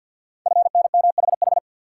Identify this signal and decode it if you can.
Morse code, heard as a keyed tone, about 48 wpm, 700 Hz. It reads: FNMHH
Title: Code Groups - 5 Letters - 50wpm